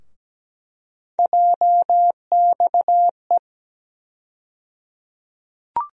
Morse code, heard as a keyed tone, about 17 wpm, 700 Hz. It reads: JXE
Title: USA Call Sign Postfixes - 3 Letters 17wpm